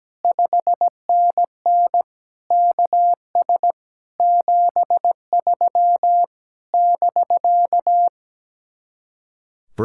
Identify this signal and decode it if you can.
Morse code, heard as a keyed tone, about 17 wpm, 700 Hz. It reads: 5NN KS 73 <BK>
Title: Speed-Racing Common QSO Phrases Encore - 17wpm